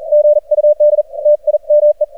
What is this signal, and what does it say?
Morse code, heard as a keyed tone, about 43 wpm, 600 Hz. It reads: GUD TIME
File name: archived / CRSnet